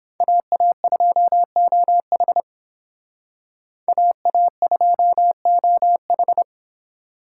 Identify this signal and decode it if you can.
Morse code, heard as a keyed tone, about 28 wpm, 700 Hz. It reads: AA2OH AA2OH